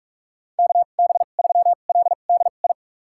Morse code, heard as a keyed tone, about 46 wpm, spaced as for 32 wpm, 700 Hz. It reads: KB3LDI
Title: USA Call Signs - 2x Character Spacing - 45wpm